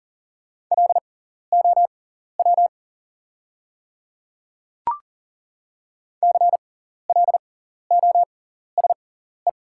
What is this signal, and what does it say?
Morse code, heard as a keyed tone, about 37 wpm, spaced as for 14 wpm, 700 Hz. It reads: LOW CLOSE
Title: 6x Character Spacing - Top 300 Words - 40wpm